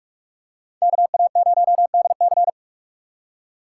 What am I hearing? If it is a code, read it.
Morse code, heard as a keyed tone, about 46 wpm, 700 Hz. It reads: KA0DC
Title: USA Call Signs - 5 Characters - 4x Character Spacing - 45wpm